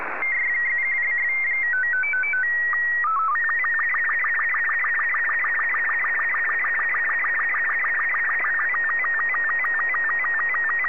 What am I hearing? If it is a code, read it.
SSTV (slow-scan television), Robot 36.
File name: sstv